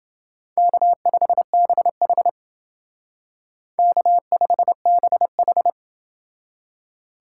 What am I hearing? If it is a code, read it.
Morse code, heard as a keyed tone, about 28 wpm, 700 Hz. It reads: K5BH K5BH